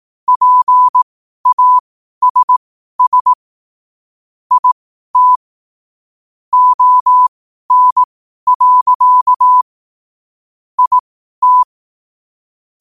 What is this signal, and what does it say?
Morse code, heard as a keyed tone, about 18 wpm, spaced as for 12 wpm, 1000 Hz. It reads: PASS IT ON. IT
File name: qotdfiles / 12wpm